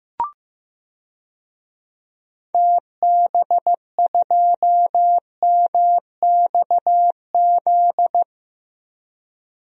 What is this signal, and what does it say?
Morse code, heard as a keyed tone, about 15 wpm, 700 Hz. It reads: TB2MXZ
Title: DXCC-100 Call Signs - 6 Characters - Repeated 2x - 15wpm